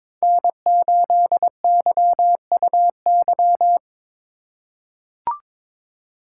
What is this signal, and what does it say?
Morse code, heard as a keyed tone, about 22 wpm, 700 Hz. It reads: N8YUY
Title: USA Call Signs - 8x Character Spacing - 22wpm